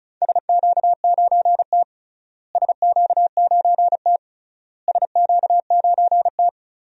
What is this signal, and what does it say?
Morse code, heard as a keyed tone, about 36 wpm, 700 Hz. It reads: SQ9T SQ9T SQ9T
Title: DXCC-100 Call Signs - 4 Characters - Repeated 3x - 35wpm